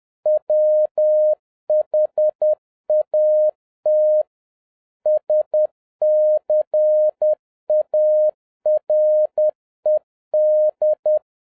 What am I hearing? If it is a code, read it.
Morse code, heard as a keyed tone, about 10 wpm, 610 Hz. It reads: WHAT SCARED